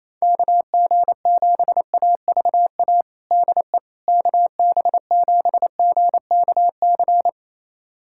Morse code, heard as a keyed tone, about 28 wpm, 700 Hz. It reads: KG7AVA DE KB7GKC
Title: USA Callsign DE Callsign - 6 Characters - 28wpm